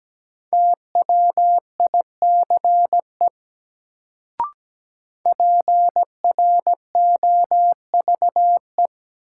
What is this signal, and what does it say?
Morse code, heard as a keyed tone, about 17 wpm, 700 Hz. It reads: TWICE PROVE
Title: Sentences from Top 1600 Words - New Word Drill Alt - 17wpm